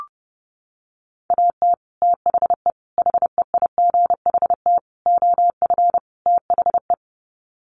Morse code, heard as a keyed tone, about 30 wpm, 700 Hz. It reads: AT THE HEIGHT OF THE
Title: Sets of 5 Words 30wpm